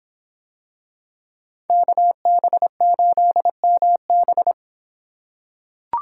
Morse code, heard as a keyed tone, about 26 wpm, 700 Hz. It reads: KB8MB